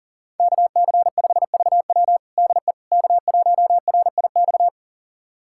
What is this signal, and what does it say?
Morse code, heard as a keyed tone, about 40 wpm, 700 Hz. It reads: KC5VW DE K1RIX
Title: USA Callsign DE Callsign 40wpm